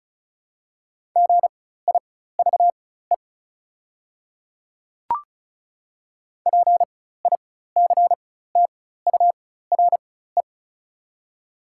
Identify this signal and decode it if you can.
Morse code, heard as a keyed tone, about 36 wpm, spaced as for 16 wpm, 690 Hz. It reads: GIVE PICTURE
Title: 4x Character Spacing - Top 200 Words - 35wpm